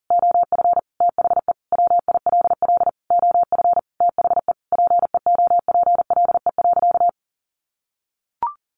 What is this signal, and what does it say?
Morse code, heard as a keyed tone, about 40 wpm, 700 Hz. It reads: OF THE WILL OF THE PEOPLE.
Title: Sentences from Top 1500 Words 40wpm